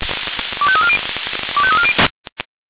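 Five-tone selective call (ZVEI), heard as quick runs of five tones, tones of about 70 ms. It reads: 25340 25340
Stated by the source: Five-tone ZVEI selective call received on an SDR.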